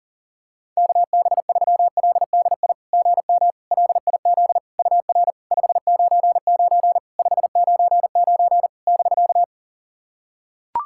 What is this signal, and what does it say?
Morse code, heard as a keyed tone, about 40 wpm, 700 Hz. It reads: KB3LDI GM LIZ UR 599 599 <BK>
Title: SOTA QSO Practice 40wpm